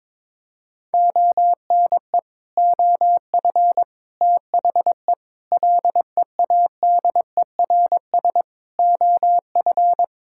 Morse code, heard as a keyed tone, about 21 wpm, 700 Hz. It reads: ONE OF THE LEADERS OF
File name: practice / 22wpm / Sets of 6 Words - 22wpm